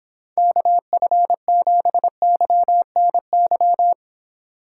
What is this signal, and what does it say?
Morse code, heard as a keyed tone, about 26 wpm, 700 Hz. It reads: KF7YNY